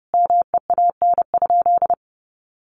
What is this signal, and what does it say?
Morse code, heard as a keyed tone, about 30 wpm, 700 Hz. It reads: MEAN?
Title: Sentences from Top 1300 Words 30wpm